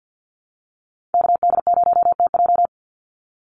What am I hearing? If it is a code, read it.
Morse code, heard as a keyed tone, about 50 wpm, 700 Hz. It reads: KD0TJ
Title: USA Call Signs 50wpm